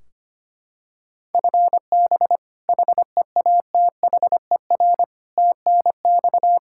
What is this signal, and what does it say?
Morse code, heard as a keyed tone, about 25 wpm, 700 Hz. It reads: FB HEATHER TNX
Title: Practice QSO - Exchange 3 - More Info - 25wpm